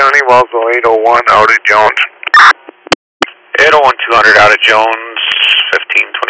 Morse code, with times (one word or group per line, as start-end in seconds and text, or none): none